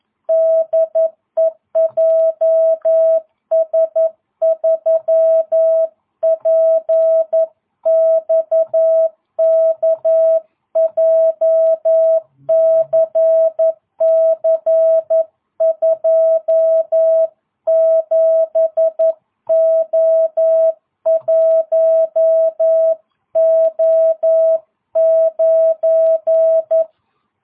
Morse code, heard as a keyed tone, about 11 wpm, 650 Hz. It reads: DE JS3PXK JCC27O1O9